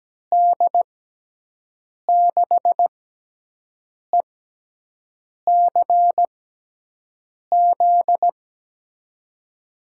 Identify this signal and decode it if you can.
Morse code, heard as a keyed tone, about 17 wpm, spaced as for 6 wpm, 700 Hz. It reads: D6ECZ
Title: USA Call Signs - 6x Character Spacing - 17wpm